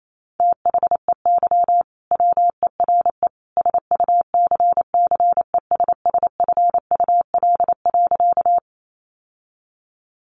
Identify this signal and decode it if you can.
Morse code, heard as a keyed tone, about 28 wpm, 700 Hz. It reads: THEY WERE SUCCESSFUL.
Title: Sentences from Top 900 Words 28wpm